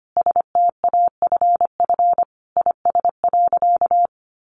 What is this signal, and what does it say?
Morse code, heard as a keyed tone, about 25 wpm, 700 Hz. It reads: STAFF IS.
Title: Sentences from Top 1200 Words 25wpm